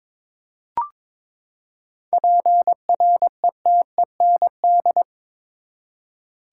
Morse code, heard as a keyed tone, about 22 wpm, 700 Hz. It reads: PRETEND
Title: Seven Letter Words - 22wpm